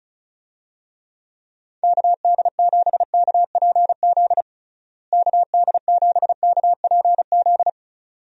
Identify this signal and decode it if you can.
Morse code, heard as a keyed tone, about 36 wpm, 700 Hz. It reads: KD7KPZ KD7KPZ
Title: USA Call Signs - 6 Characters - Repeated 2x - 35wpm